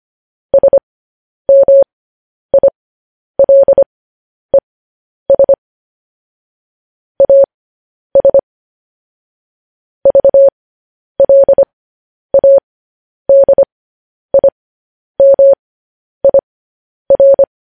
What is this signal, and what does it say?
Morse code, heard as a keyed tone, about 25 wpm, spaced as for 10 wpm, 550 Hz. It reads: SMILES AS VLADIMIR
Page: News Headlines in Morse Code at 10 WPM